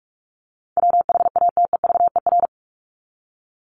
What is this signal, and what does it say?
Morse code, heard as a keyed tone, about 44 wpm, 700 Hz. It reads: WHATEVER
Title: Top 1001-1250 Words 45wpm